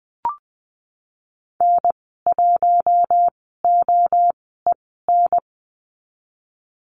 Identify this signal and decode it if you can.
Morse code, heard as a keyed tone, about 20 wpm, spaced as for 14 wpm, 700 Hz. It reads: N1OEN
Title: 042 Lesson 8g - Number 1 - Call Signs 20wpm